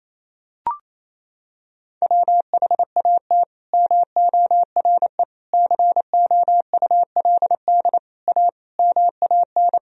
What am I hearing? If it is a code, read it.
Morse code, heard as a keyed tone, about 28 wpm, 700 Hz. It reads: WHAT MORE COULD A MAN
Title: Sentences from Top 200 Words and Abbreviations 28wpm